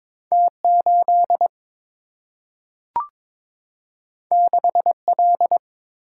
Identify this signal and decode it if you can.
Morse code, heard as a keyed tone, about 22 wpm, 700 Hz. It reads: T 8 6 L
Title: Single-Letter Single-Number 22wpm